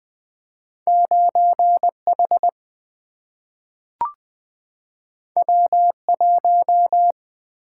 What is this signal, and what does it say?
Morse code, heard as a keyed tone, about 20 wpm, 700 Hz. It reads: 9H W1
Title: Single-Letter Single-Number 20wpm